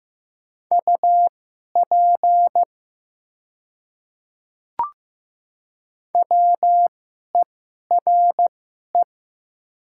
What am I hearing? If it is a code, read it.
Morse code, heard as a keyed tone, about 15 wpm, spaced as for 11 wpm, 700 Hz. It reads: UP WERE
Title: 2x Character Spacing - Top 200 Words - 15wpm